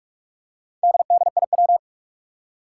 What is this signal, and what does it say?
Morse code, heard as a keyed tone, about 47 wpm, 700 Hz. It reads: DDIW
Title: Four Challenging Letter-Number Combinations 45wpm